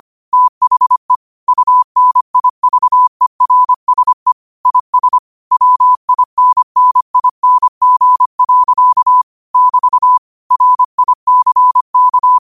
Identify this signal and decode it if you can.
Morse code, heard as a keyed tone, about 25 wpm, 1000 Hz. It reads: THE UNIVERSE IS WINNING. <BT> RICK
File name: qotdfiles / 25wpm